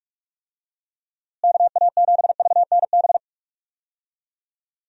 Morse code, heard as a keyed tone, about 46 wpm, 700 Hz. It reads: KA7VNB
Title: USA Call Signs - 6 Characters - Repeated 3x - 45wpm